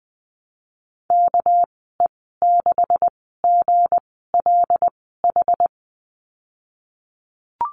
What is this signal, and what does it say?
Morse code, heard as a keyed tone, about 20 wpm, spaced as for 15 wpm, 700 Hz. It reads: KE6GLH
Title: 231 Lesson 37g - Letter x - Call Signs 20wpm